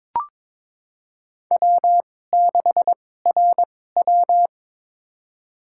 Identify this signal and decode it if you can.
Morse code, heard as a keyed tone, about 22 wpm, spaced as for 16 wpm, 700 Hz. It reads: W6RW
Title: USA Call Signs - 4 Characters - 2x Character Spacing - 22wpm